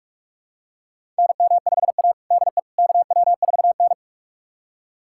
Morse code, heard as a keyed tone, about 43 wpm, 700 Hz. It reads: NM5A DE KW4N
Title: USA Callsign DE Callsign - 4 Characters - 45wpm